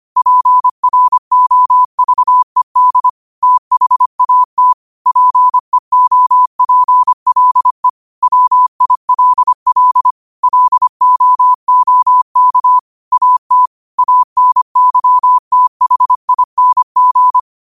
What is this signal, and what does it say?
Morse code, heard as a keyed tone, about 25 wpm, 1000 Hz. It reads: PROVED THAT PEOPLE WILL LOOK AT ANYTHING